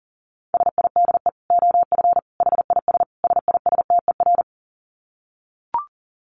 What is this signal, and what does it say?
Morse code, heard as a keyed tone, about 40 wpm, 700 Hz. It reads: SIDE OF HIS SISTER
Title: Dit Challenge 40wpm